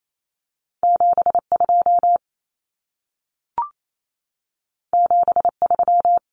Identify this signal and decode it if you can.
Morse code, heard as a keyed tone, about 28 wpm, 700 Hz. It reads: 72 73
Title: Numbers - 2-Digit 28wpm